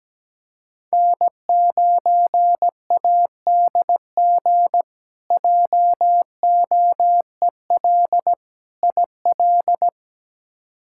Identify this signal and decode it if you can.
Morse code, heard as a keyed tone, about 17 wpm, 700 Hz. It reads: N9ADG JOEL IL
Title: K1USN SST QSO Practice - 17wpm